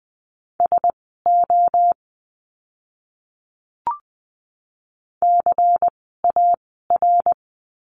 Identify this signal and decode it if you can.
Morse code, heard as a keyed tone, about 20 wpm, spaced as for 15 wpm, 700 Hz. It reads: SO CAR
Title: 108 Lesson 18f - Letter m - Words 20wpm